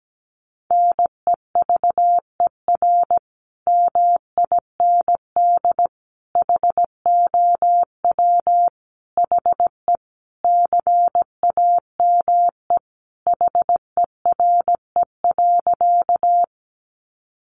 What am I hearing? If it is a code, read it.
Morse code, heard as a keyed tone, about 17 wpm, 700 Hz. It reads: NEVER MIND HOW HE CAME HERE.
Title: Sentences from Top 500 Words 17wpm